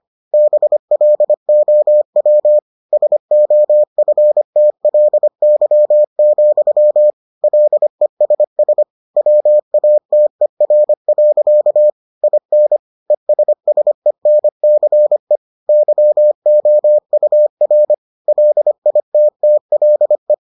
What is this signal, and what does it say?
Morse code, heard as a keyed tone, about 25 wpm, 600 Hz. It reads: BLOW SOFTLY, LESS WATER. IN ESSENCE YOUR LITTLE